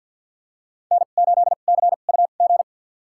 Morse code, heard as a keyed tone, about 53 wpm, spaced as for 37 wpm, 700 Hz. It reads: N8CUG
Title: USA Call Signs - 5 Characters - 2x Character Spacing - 50wpm